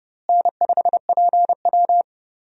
Morse code, heard as a keyed tone, about 32 wpm, 700 Hz. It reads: N5PW
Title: USA Call Signs - 4 Characters - Repeated 2x - 30wpm